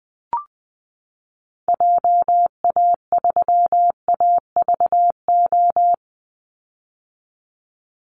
Morse code, heard as a keyed tone, about 20 wpm, 700 Hz. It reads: JA3AVO
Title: Call Signs - 20wpm